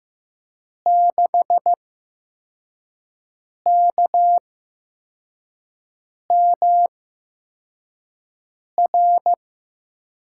Morse code, heard as a keyed tone, about 15 wpm, spaced as for 4 wpm, 700 Hz. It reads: 6KMR
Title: USA Call Signs - 6 Characters - 8x Character Spacing - 15wpm